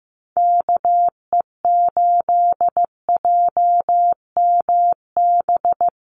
Morse code, heard as a keyed tone, about 14 wpm, 700 Hz. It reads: KE8JMB
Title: USA Call Signs Repeated 3x 15wpm